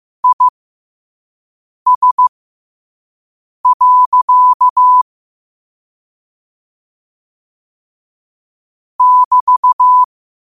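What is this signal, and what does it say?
Morse code, heard as a keyed tone, about 15 wpm, spaced as for 5 wpm, 1000 Hz. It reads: IS. <BT>
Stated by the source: Morse code quote of the day sent at 5 wpm.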